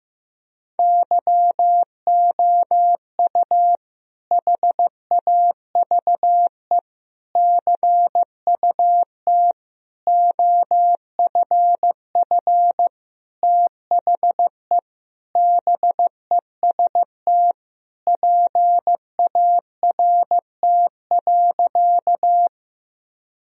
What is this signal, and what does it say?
Morse code, heard as a keyed tone, about 15 wpm, 700 Hz. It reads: YOU HAVE CUT OFF THE BEST PART.
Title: Sentences from Top 1000 Words - Review Part 1 - 15wpm